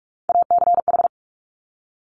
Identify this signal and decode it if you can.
Morse code, heard as a keyed tone, about 46 wpm, 700 Hz. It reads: ACH
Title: Top Three Letter Combinations in Words 45wpm